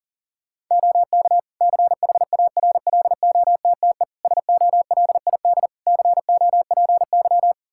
Morse code, heard as a keyed tone, about 40 wpm, 700 Hz. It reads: OK CHARLOTTE SOLID COPY
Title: Practice QSO - Exchange 3 - More Info - 40wpm